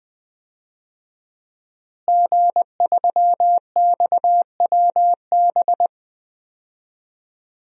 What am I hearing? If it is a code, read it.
Morse code, heard as a keyed tone, about 21 wpm, 700 Hz. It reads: G3XWB
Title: DXCC-100 Call Signs - 5 Characters - Repeated 3x - 20wpm